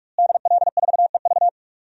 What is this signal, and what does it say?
Morse code, heard as a keyed tone, about 46 wpm, 700 Hz. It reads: DL4EV
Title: DXCC-100 Call Signs - 5 Characters - Repeated 3x - 45wpm